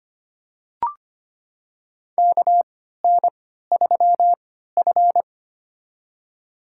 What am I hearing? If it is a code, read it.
Morse code, heard as a keyed tone, about 25 wpm, spaced as for 14 wpm, 700 Hz. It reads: KN3F
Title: USA Call Signs - 4 Characters - 3x Character Spacing - 25wpm